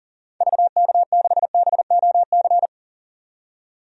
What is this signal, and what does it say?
Morse code, heard as a keyed tone, about 38 wpm, 700 Hz. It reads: VK6BOC
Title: DXCC-100 Call Signs - 6 Characters - 40wpm